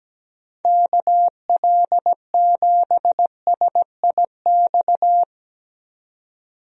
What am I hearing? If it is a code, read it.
Morse code, heard as a keyed tone, about 17 wpm, 700 Hz. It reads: KL7SIX
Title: DXCC-100 Call Signs - 6 Characters - 17wpm